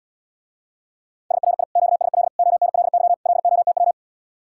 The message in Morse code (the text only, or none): SRI QSL ONLY LOTW